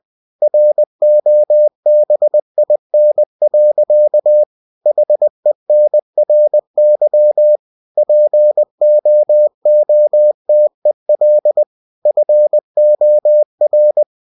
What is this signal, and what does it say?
Morse code, heard as a keyed tone, about 20 wpm, 600 Hz. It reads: ROBIN. HENRY POOTEL FOR
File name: cw-books / winnie-the-pooh / winnie-20wpm-page-0074